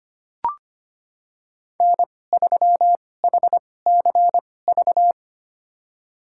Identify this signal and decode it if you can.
Morse code, heard as a keyed tone, about 25 wpm, spaced as for 18 wpm, 700 Hz. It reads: N3HCV
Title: USA Call Signs - 5 Characters - 2x Character Spacing - 25wpm